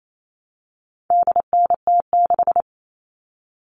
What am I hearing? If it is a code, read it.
Morse code, heard as a keyed tone, about 28 wpm, 700 Hz. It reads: DNT6
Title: All Cut Numbers - 4-Digit 28wpm